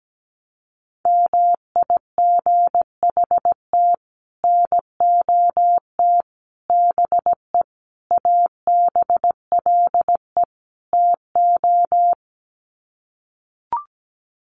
Morse code, heard as a keyed tone, about 17 wpm, 700 Hz. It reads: MIGHT NOT BE ABLE TO
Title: Sets of 5 Words Encore2 17wpm